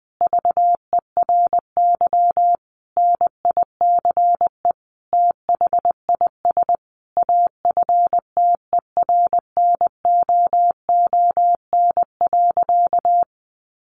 Morse code, heard as a keyed tone, about 19 wpm, 700 Hz. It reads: VERY NICE THIS AFTERNOON.
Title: Sentences from Top 1500 Words 20wpm